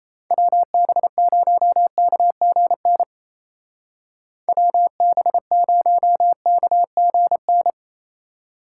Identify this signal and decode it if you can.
Morse code, heard as a keyed tone, about 34 wpm, 700 Hz. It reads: WB0KGN WB0KGN
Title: Speed Racing - USA Call Signs 28wpm